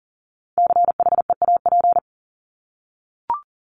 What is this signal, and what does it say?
Morse code, heard as a keyed tone, about 40 wpm, 700 Hz. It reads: CHEAP
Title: Top 1001-1250 Words 40wpm